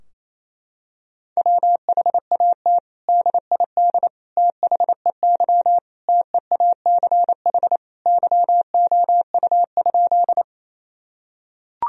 Morse code, heard as a keyed tone, about 28 wpm, 700 Hz. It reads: WHAT DID THEY TEACH YOU?
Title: Sentences from Top 1000 Words - Review Part 2 - 28wpm